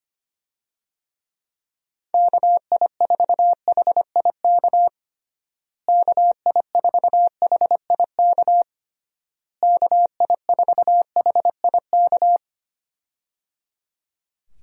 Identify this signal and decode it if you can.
Morse code, heard as a keyed tone, about 25 wpm, 700 Hz. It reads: KI4HIK KI4HIK KI4HIK
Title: USA Call Signs - 6 Characters - Repeated 3x - 25wpm